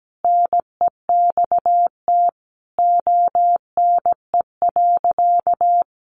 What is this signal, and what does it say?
Morse code, heard as a keyed tone, about 17 wpm, 700 Hz. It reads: NEXT ONE.
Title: Sentences from Top 400 Words 17wpm